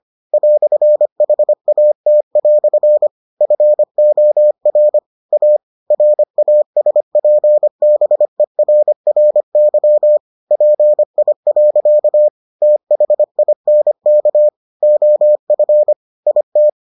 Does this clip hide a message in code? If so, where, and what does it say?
Yes: Morse code from 0.3 to 16.7 s, heard as a keyed tone, about 25 wpm, 600 Hz. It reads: "HAT" FOR A RASPBERRY PI. THINK OF IT